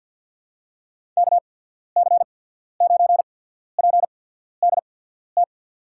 Morse code, heard as a keyed tone, about 53 wpm, spaced as for 14 wpm, 700 Hz. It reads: KC9PDT
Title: USA Call Signs - 6 Characters - 8x Character Spacing - 50wpm